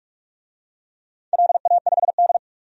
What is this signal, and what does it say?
Morse code, heard as a keyed tone, about 43 wpm, 700 Hz. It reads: LA5D
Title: DXCC-100 Call Signs - 4 Characters - 45wpm